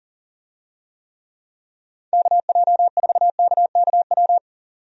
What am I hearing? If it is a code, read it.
Morse code, heard as a keyed tone, about 40 wpm, 700 Hz. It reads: KJ4KKW
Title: USA Call Signs - 6 Characters - 40wpm